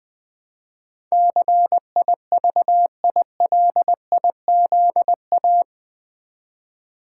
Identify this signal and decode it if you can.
Morse code, heard as a keyed tone, about 20 wpm, 700 Hz. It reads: CIVILIZA
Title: Build A Word - 4 Letters At A Time - 12-Letter Words - 20wpm